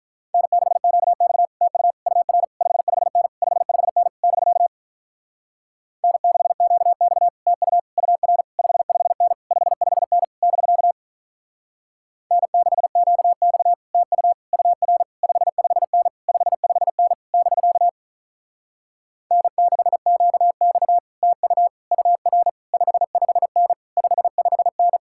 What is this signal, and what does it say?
Morse code, heard as a keyed tone, about 47 wpm, 700 Hz. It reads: N6QX TU UR 55N 55N <BK> N6QX TU UR 55N 55N <BK> N6QX TU UR 55N 55N <BK> N6QX TU UR 55N 55N